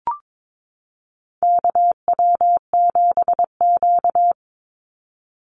Morse code, heard as a keyed tone, about 22 wpm, 700 Hz. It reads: KW7Q
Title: Call Signs Repeated 3x - 22wpm